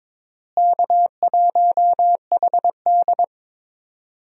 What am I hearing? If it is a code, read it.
Morse code, heard as a keyed tone, about 22 wpm, 700 Hz. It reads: K1HD
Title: USA Call Signs - 4 Characters - Repeated 3x - 22wpm